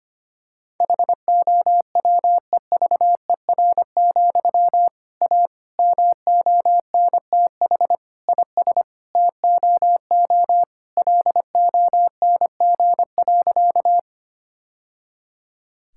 Morse code, heard as a keyed tone, about 25 wpm, 700 Hz. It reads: HOWEVER, A MONTH IS TOO LONG.
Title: Sentences from Top 1000 Words - Review Part 5 - 25wpm